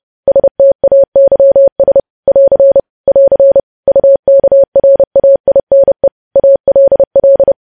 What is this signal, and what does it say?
Morse code, heard as a keyed tone, about 30 wpm, 550 Hz. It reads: STAYS <AR> <AR> UKRAINE ALL
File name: podcast / News headlines-1x-30x30